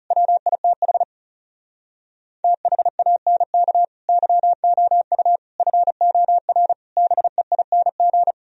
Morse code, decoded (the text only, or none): WITH THANK YOU FOR BEING